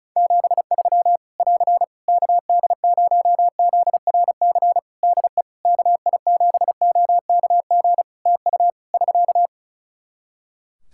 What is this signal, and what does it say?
Morse code, heard as a keyed tone, about 34 wpm, 700 Hz. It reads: 73 <AR> KD0ZRC DE KI7OKG TU <SK>
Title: K1USN SST QSO Practice 35wpm